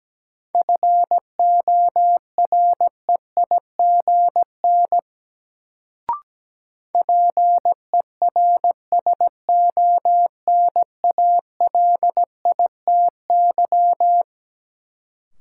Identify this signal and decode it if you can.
Morse code, heard as a keyed tone, about 17 wpm, 700 Hz. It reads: FOREIGN PERSONALITY
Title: Sentences from Top 1400 Words - New Word Drill - 17wpm